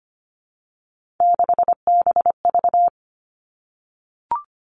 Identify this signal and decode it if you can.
Morse code, heard as a keyed tone, about 25 wpm, 700 Hz. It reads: 6BV
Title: Three Challenging Letter-Number Combinations 25wpm